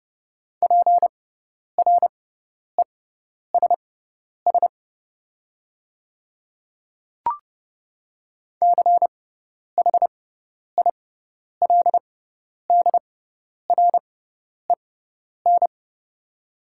Morse code, heard as a keyed tone, about 30 wpm, spaced as for 10 wpm, 700 Hz. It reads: PRESS CHILDREN
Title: 6x Character Spacing - Top 300 Words - 30wpm